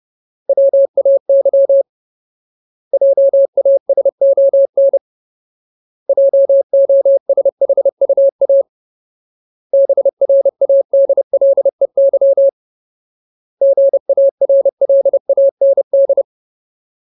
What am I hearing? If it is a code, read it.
Morse code, heard as a keyed tone, about 30 wpm, 550 Hz. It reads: WAY JASON JOSHUA BRADLEY GARLAND
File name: Ultimate Word List / 30WPM Ultimate